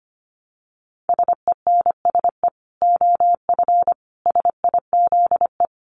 Morse code, heard as a keyed tone, about 25 wpm, 700 Hz. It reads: SENSE OF SIZE
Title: Dit Challenge 25wpm